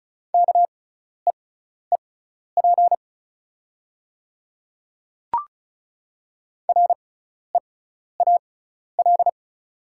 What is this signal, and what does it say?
Morse code, heard as a keyed tone, about 34 wpm, spaced as for 12 wpm, 700 Hz. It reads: KEEP REAL
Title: 6x Character Spacing - Top 300 Words - 35wpm